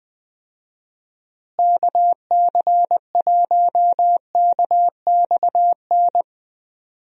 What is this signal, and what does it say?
Morse code, heard as a keyed tone, about 21 wpm, 700 Hz. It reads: KC1KXN
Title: USA Call Signs - 6 Characters - Repeated 2x - 20wpm